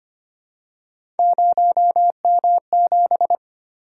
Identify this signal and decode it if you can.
Morse code, heard as a keyed tone, about 25 wpm, 700 Hz. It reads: 0M7
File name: Three Letter-Number Combinations - 25wpm